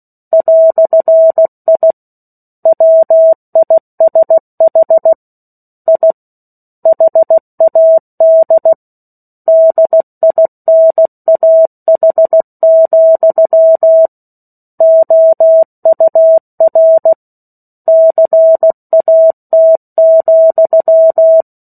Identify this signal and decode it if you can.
Morse code, heard as a keyed tone, about 16 wpm, 650 Hz. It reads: "I WISH I HAD DINAH, OUR CAT,
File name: 03-Chapter-03 16 WPM